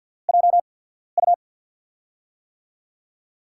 Morse code, heard as a keyed tone, about 48 wpm, spaced as for 14 wpm, 700 Hz. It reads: JU
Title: USA Call Signs - 6 Characters - 8x Character Spacing - 50wpm